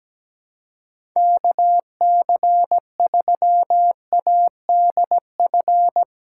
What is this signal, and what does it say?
Morse code, heard as a keyed tone, about 17 wpm, 700 Hz. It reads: KC3ADF
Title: USA Callsign DE Callsign 17wpm